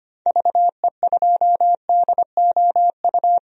Morse code, heard as a keyed tone, about 25 wpm, 700 Hz. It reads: VE2DOU
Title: DXCC-100 Call Signs - 6 Characters - 25wpm